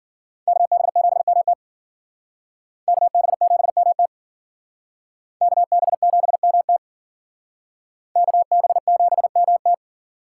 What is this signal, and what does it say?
Morse code, heard as a keyed tone, about 52 wpm, 700 Hz. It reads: KB7MT KB7MT KB7MT KB7MT